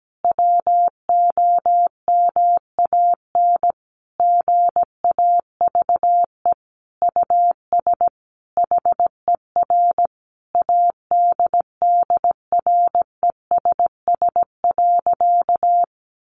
Morse code, heard as a keyed tone, about 17 wpm, 700 Hz. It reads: WOMAN GAVE US HER ADDRESS.
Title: Sentences from Top 1100 Words 17wpm